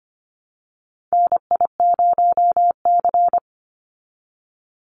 Morse code, heard as a keyed tone, about 25 wpm, 700 Hz. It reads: NI0C
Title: Call Signs Repeated 3x - 25wpm